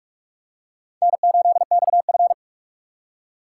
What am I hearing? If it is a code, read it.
Morse code, heard as a keyed tone, about 43 wpm, 700 Hz. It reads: N8XF